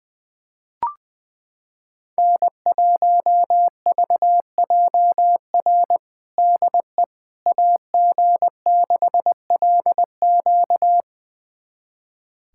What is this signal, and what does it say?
Morse code, heard as a keyed tone, about 21 wpm, 700 Hz. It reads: N1VJR DE AG6LQ
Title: USA Callsign DE Callsign - 5 Characters - 20wpm